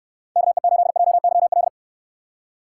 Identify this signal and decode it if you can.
Morse code, heard as a keyed tone, about 69 wpm, 700 Hz. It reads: KE8JYL